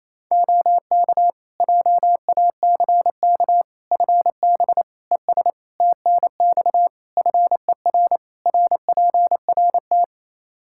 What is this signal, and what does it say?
Morse code, heard as a keyed tone, about 28 wpm, 700 Hz. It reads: OK JACK FB ES TNX FER RPRT
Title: Practice QSO - Exchange 2 - Info - 28wpm